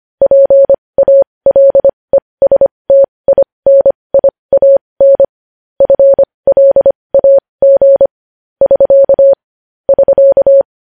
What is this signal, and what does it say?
Morse code, heard as a keyed tone, about 25 wpm, spaced as for 20 wpm, 550 Hz. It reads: PALESTINIAN FLAG <SK> <SK>
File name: podcast / News headlines-1x-25x20